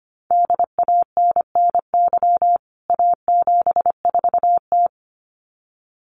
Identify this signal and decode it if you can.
Morse code, heard as a keyed tone, about 25 wpm, 700 Hz. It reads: DANNY A74T
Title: CWT Exchange 25wpm